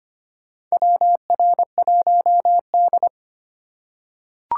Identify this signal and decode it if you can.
Morse code, heard as a keyed tone, about 25 wpm, 700 Hz. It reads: WR1D
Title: USA Call Signs - 4 Characters - 3x Character Spacing - 25wpm